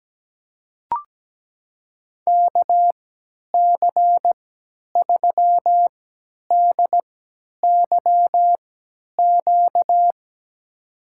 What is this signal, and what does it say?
Morse code, heard as a keyed tone, about 17 wpm, spaced as for 10 wpm, 700 Hz. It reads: KC3DYQ
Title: USA Call Signs - 6 Characters - 3x Character Spacing - 17wpm